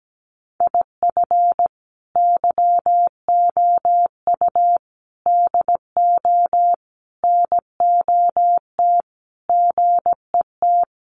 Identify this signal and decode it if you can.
Morse code, heard as a keyed tone, about 17 wpm, 700 Hz. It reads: IF YOU DO NOT GET
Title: Sets of 5 Words Encore2 17wpm